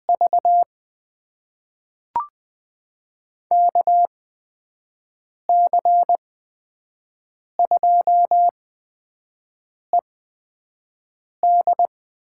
Morse code, heard as a keyed tone, about 20 wpm, spaced as for 5 wpm, 700 Hz. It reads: V KC2ED
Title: USA Call Signs - 6 Characters - 8x Character Spacing - 20wpm